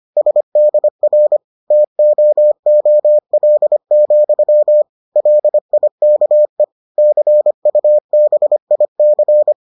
Morse code, heard as a keyed tone, about 25 wpm, 600 Hz. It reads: SDR TOOL, LIKE CUBIC